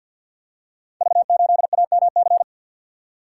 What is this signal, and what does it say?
Morse code, heard as a keyed tone, about 50 wpm, 700 Hz. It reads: V8AMC
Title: Code Groups - 5 Letter:Numbers - 50wpm